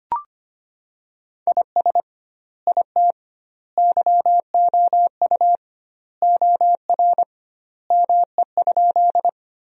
Morse code, heard as a keyed tone, about 25 wpm, 700 Hz. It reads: IS IT YOU OR ME?
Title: 2x Word-Spacing - Sentences from Top 200 Words - 25wpm